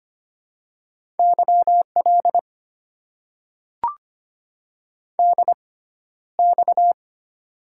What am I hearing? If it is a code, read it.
Morse code, heard as a keyed tone, about 25 wpm, spaced as for 13 wpm, 690 Hz. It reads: YL DX
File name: Top 90 QSO Elements - 6x Character Spacing - 25wpm